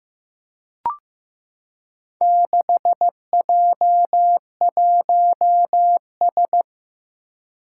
Morse code, heard as a keyed tone, about 15 wpm, 700 Hz. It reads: 6J1S
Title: Four Challenging Letter-Number Combinations 15wpm